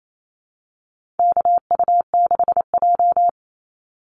Morse code, heard as a keyed tone, about 28 wpm, 700 Hz. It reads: KU6J
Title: NA SOTA Activator Callsigns 28wpm